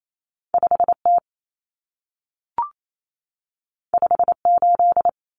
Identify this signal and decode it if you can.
Morse code, heard as a keyed tone, about 28 wpm, 700 Hz. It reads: 5T 58
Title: Common Cut Numbers - 2-Digit 28wpm